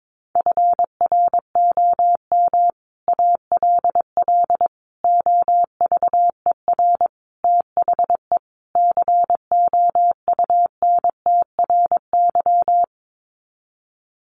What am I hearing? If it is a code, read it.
Morse code, heard as a keyed tone, about 22 wpm, 700 Hz. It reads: FROM ALL OVER THE COUNTRY
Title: Sets of 5 Words Encore 22wpm